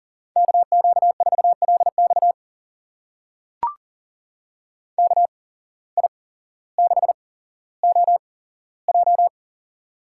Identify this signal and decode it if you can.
Morse code, heard as a keyed tone, about 42 wpm, spaced as for 17 wpm, 700 Hz. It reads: KQ4LX KI6OJ
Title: USA Call Signs - 5 Characters - 8x Character Spacing - 40wpm